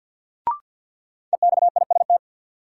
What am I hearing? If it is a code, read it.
Morse code, heard as a keyed tone, about 53 wpm, 700 Hz. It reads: EXIST
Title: Sentences from Top 1100 Words - New Word Drill Alt - 50wpm